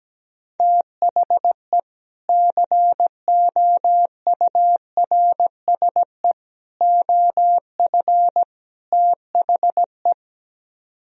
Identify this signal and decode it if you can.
Morse code, heard as a keyed tone, about 17 wpm, 700 Hz. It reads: THE COURSE OF THE
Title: Speed-Racing - Sets of 4 Words 17wpm